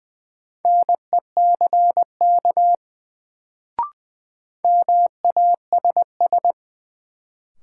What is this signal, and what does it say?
Morse code, heard as a keyed tone, about 20 wpm, 700 Hz. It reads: NECK MASS
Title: Sets of 5 Words Encore - New Word Drill - 20wpm